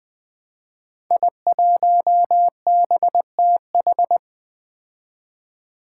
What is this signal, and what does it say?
Morse code, heard as a keyed tone, about 21 wpm, 700 Hz. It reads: I1BTH
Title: Five Challenging Letter-Number Combinations 20wpm